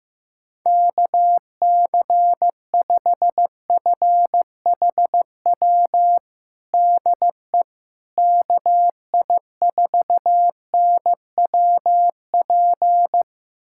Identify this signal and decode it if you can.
Morse code, heard as a keyed tone, about 15 wpm, 700 Hz. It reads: KC5FHW DE KI4NWP
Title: USA Callsign DE Callsign 15wpm